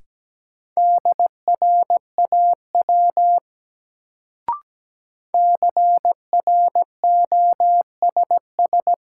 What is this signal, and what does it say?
Morse code, heard as a keyed tone, about 17 wpm, 700 Hz. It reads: DRAW CROSS
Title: Sentences from Top 300 Words - New Word Drill - 17wpm